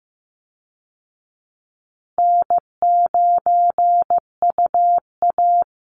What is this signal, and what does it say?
Morse code, heard as a keyed tone, about 15 wpm, 700 Hz. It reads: N9UA
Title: Easy Call Signs Repeated 2x - 15wpm